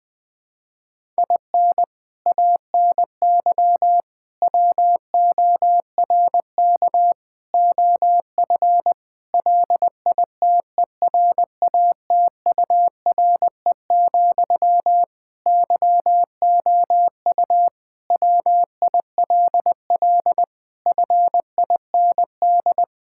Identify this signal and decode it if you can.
Morse code, heard as a keyed tone, about 20 wpm, 700 Hz. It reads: IN ANY WORK OF LITERATURE, YOU WILL FIND
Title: Sentences from Top 1000 Words - Review Part 6 - 20wpm